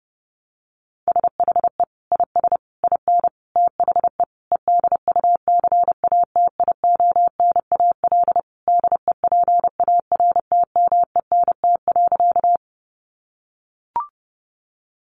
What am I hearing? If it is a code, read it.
Morse code, heard as a keyed tone, about 30 wpm, 700 Hz. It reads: SHE IS IN THE EDUCATIONAL DEPARTMENT.
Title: Sentences from Top 1200 Words 30wpm